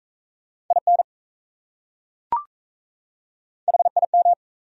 Morse code, heard as a keyed tone, about 37 wpm, 700 Hz. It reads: IN HIM